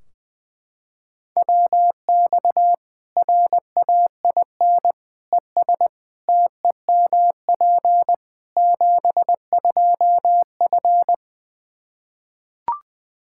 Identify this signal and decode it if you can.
Morse code, heard as a keyed tone, about 20 wpm, 700 Hz. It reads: WX RAIN ES TEMP 72F
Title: K1USN SST QSO Practice 20wpm